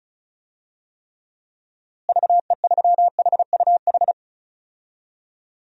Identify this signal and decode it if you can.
Morse code, heard as a keyed tone, about 36 wpm, 700 Hz. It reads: VE3HUH
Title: DXCC-100 Call Signs - 6 Characters - Repeated 3x - 35wpm